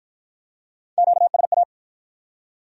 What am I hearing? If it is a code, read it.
Morse code, heard as a keyed tone, about 53 wpm, 700 Hz. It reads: QSA